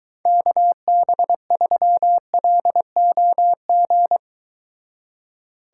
Morse code, heard as a keyed tone, about 23 wpm, 700 Hz. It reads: KB3LOG